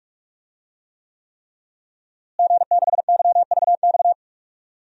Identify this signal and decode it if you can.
Morse code, heard as a keyed tone, about 45 wpm, 700 Hz. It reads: G6YVX
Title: DXCC-100 Call Signs - 5 Characters - Repeated 3x - 45wpm